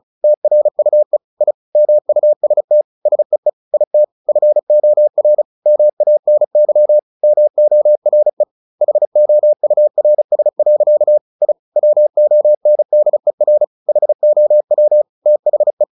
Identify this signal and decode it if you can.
Morse code, heard as a keyed tone, about 34 wpm, 600 Hz. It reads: TRUE I MUST SEE IT FOR MANY MORE HOURS. I WONDER HOW THE
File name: the old man and the sea / the old man and the sea 1 35wpm